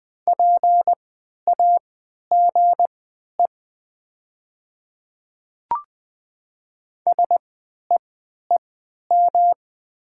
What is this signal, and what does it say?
Morse code, heard as a keyed tone, about 20 wpm, spaced as for 11 wpm, 690 Hz. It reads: PAGE SEEM
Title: 3x Character Spacing - Top 300 Words - 20wpm